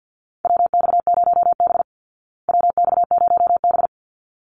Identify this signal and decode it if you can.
Morse code, heard as a keyed tone, about 48 wpm, 700 Hz. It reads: WX0B WX0B
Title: Call Signs Repeated 3x - 50wpm